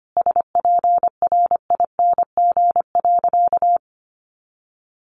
Morse code, heard as a keyed tone, about 25 wpm, 700 Hz. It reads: SPRING.
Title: Sentences from Top 1200 Words 25wpm